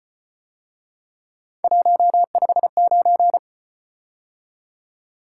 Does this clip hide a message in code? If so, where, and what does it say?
Yes: Morse code from 1.6 to 3.4 s, heard as a keyed tone, about 33 wpm, 700 Hz. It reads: 159